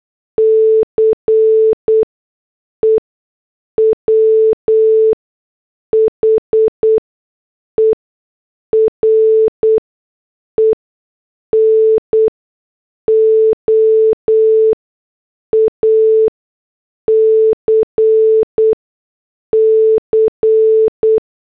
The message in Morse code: CE WHERE NO ACC